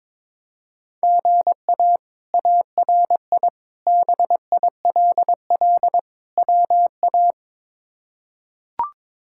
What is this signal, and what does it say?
Morse code, heard as a keyed tone, about 22 wpm, 700 Hz. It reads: GA ARI BILL WA
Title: K1USN SST QSO Practice - 22wpm